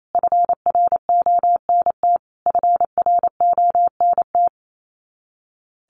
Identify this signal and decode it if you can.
Morse code, heard as a keyed tone, about 28 wpm, 700 Hz. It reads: FRONT FRONT
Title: Top 500 Words Repeated 28wpm